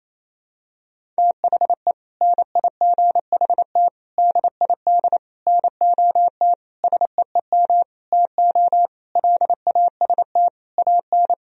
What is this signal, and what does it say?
Morse code, heard as a keyed tone, about 28 wpm, 700 Hz. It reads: THE NIGHT DID NOT SEEM TO LAST AN
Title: Sentences from Top 1000 Words - Review Part 1 - 28wpm